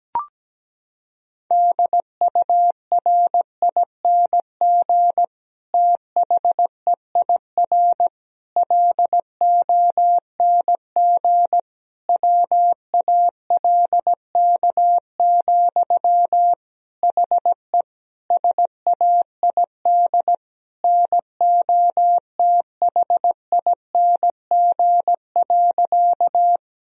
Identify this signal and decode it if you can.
Morse code, heard as a keyed tone, about 17 wpm, 700 Hz. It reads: DURING THEIR LONG WALK, HE SAID NOTHING.
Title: Sentences from Top 1000 Words - Review Part 2 - 17wpm